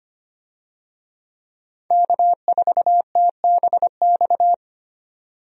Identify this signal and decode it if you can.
Morse code, heard as a keyed tone, about 25 wpm, 700 Hz. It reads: K4TBX
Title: USA Call Signs - 5 Characters - 25wpm